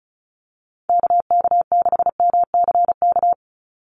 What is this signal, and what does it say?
Morse code, heard as a keyed tone, about 34 wpm, 700 Hz. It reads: KK6MCK
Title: USA Call Signs 35wpm